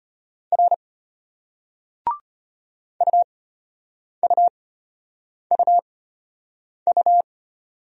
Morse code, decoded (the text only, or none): R UUUU